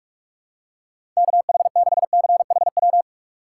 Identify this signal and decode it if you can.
Morse code, heard as a keyed tone, about 45 wpm, 700 Hz. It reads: KH6CHW
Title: USA Call Signs - 6x Character Spacing - 45wpm